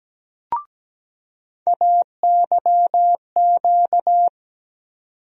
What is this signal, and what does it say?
Morse code, heard as a keyed tone, about 17 wpm, 700 Hz. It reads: AYQ
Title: USA Call Sign Postfixes - 3 Letters 17wpm